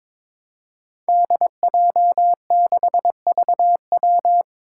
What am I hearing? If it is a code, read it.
Morse code, heard as a keyed tone, about 22 wpm, 700 Hz. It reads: DJ6VW
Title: Five Challenging Letter-Number Combinations 22wpm